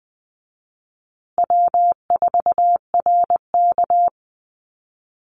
Morse code, heard as a keyed tone, about 20 wpm, 700 Hz. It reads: W4RK
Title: Easy Call Signs Repeated 3x - 20wpm